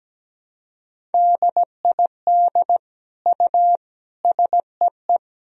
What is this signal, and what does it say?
Morse code, heard as a keyed tone, about 17 wpm, 700 Hz. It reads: DID U SEE
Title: Sentences from Top 100 Words and Abbreviations 17wpm